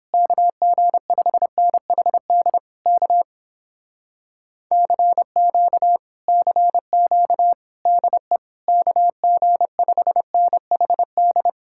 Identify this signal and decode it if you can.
Morse code, heard as a keyed tone, about 29 wpm, 700 Hz. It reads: KG5NHD K CQ CQ DE KG5NHD